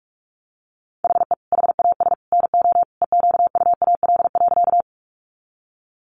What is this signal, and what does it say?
Morse code, heard as a keyed tone, about 44 wpm, 700 Hz. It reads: HE HAS NO EQUAL.
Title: Sentences from Top 1400 Words 45wpm